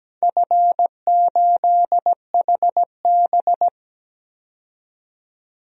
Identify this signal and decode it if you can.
Morse code, heard as a keyed tone, about 17 wpm, 700 Hz. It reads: F8HB
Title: DXCC-100 Call Signs - 4 Characters - 17wpm